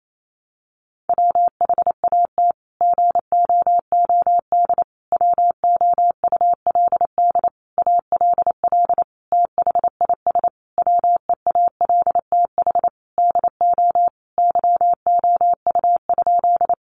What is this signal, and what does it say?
Morse code, heard as a keyed tone, about 28 wpm, 700 Hz. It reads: WHAT GOOD WOULD ALL THIS WEALTH DO YOU?
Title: Sentences from Top 1200 Words 28wpm